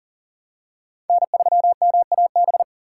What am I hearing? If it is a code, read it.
Morse code, heard as a keyed tone, about 40 wpm, 700 Hz. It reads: N3MAB
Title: USA Call Signs - 5 Characters - 3x Character Spacing - 40wpm